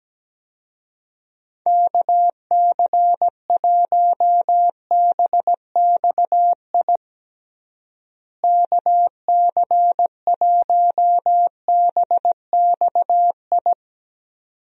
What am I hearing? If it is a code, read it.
Morse code, heard as a keyed tone, about 17 wpm, 700 Hz. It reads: KC1BXI KC1BXI
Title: USA Call Signs - 6 Characters - Repeated 2x - 17wpm